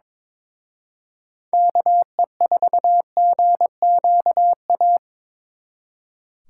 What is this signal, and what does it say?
Morse code, heard as a keyed tone, about 27 wpm, 700 Hz. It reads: KE4GQA
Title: Speed Racing - USA 2x3 Call Signs 22wpm